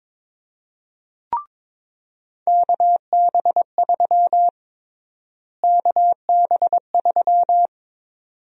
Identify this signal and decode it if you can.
Morse code, heard as a keyed tone, about 22 wpm, 700 Hz. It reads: KB3 KB3
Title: USA Call Sign Prefixes - Repeated 2x - 22wpm